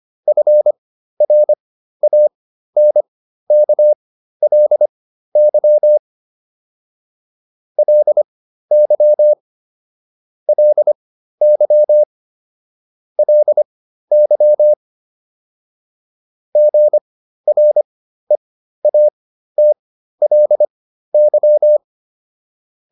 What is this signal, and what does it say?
Morse code, heard as a keyed tone, about 25 wpm, spaced as for 13 wpm, 600 Hz. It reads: FRANKLY LY LY LY GREATLY